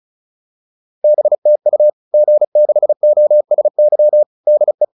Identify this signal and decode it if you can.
Morse code, heard as a keyed tone, about 34 wpm, 600 Hz. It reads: BTU G6OSY DE